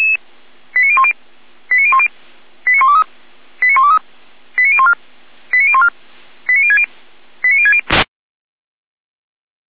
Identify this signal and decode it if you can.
Five-tone selective call (ZVEI), heard as quick runs of five tones, tones of about 70 ms. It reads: 89010 89010 89123 89123 89014 89014 89060 89060